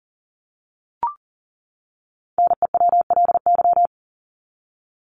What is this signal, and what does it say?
Morse code, heard as a keyed tone, about 40 wpm, 700 Hz. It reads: NEWLY
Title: LY Suffix - Top 100 Words with Base Words 40wpm